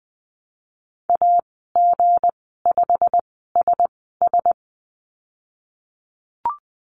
Morse code, heard as a keyed tone, about 20 wpm, spaced as for 15 wpm, 700 Hz. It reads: AG5SS
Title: 167 Lesson 27g - Number 7 - Call Signs 20wpm